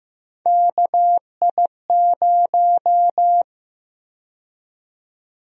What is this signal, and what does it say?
Morse code, heard as a keyed tone, about 15 wpm, 700 Hz. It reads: KI0
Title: USA Call Sign Prefixes - Repeated 2x - 15wpm